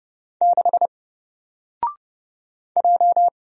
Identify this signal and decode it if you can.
Morse code, heard as a keyed tone, about 30 wpm, 700 Hz. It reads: 6 J
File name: Single-Letter-Number-30wpm